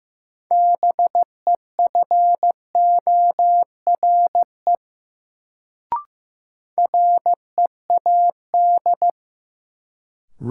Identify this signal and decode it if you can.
Morse code, heard as a keyed tone, about 15 wpm, 700 Hz. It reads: BEFORE READ
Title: Sentences from Top 200 Words - New Word Drill - 15wpm